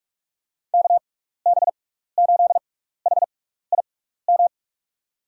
Morse code, heard as a keyed tone, about 45 wpm, spaced as for 15 wpm, 700 Hz. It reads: KB8HIM
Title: USA Call Signs - 6 Characters - 6x Character Spacing - 45wpm